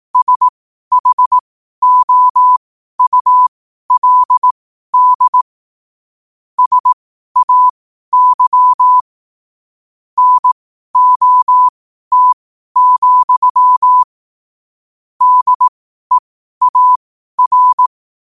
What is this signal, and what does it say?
Morse code, heard as a keyed tone, about 18 wpm, spaced as for 12 wpm, 1000 Hz. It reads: SHOULD SAY NOT, DEAR
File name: qotdfiles / 12wpm